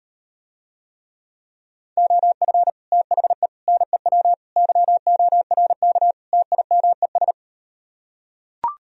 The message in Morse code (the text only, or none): OF THE NEW YORK TIMES